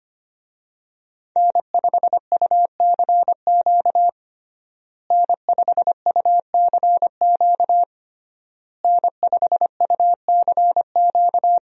Morse code, heard as a keyed tone, about 25 wpm, 700 Hz. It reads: N5UCQ N5UCQ N5UCQ
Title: USA Call Signs - 5 Characters - Repeated 3x - 25wpm